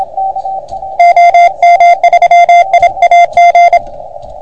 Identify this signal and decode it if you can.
Morse code, heard as a keyed tone, about 30 wpm, 690 Hz, with radio noise. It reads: OM3IAG
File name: Archiv / CUC381 / OK1FGD CUC 381